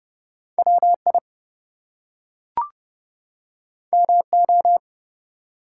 Morse code, heard as a keyed tone, about 30 wpm, 700 Hz. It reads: WI MO
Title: US States and Canadian Provinces 30wpm